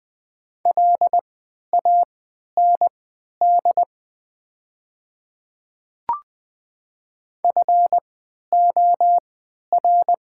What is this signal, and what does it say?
Morse code, heard as a keyed tone, about 20 wpm, spaced as for 11 wpm, 700 Hz. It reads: LAND FOR
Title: 3x Character Spacing - Top 500 Words - 20wpm